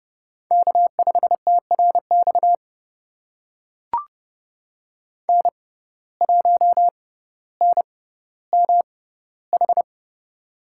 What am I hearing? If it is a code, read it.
Morse code, heard as a keyed tone, about 30 wpm, spaced as for 15 wpm, 700 Hz. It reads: K5TRX N1NMH
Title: USA Call Signs - 5 Characters - 6x Character Spacing - 30wpm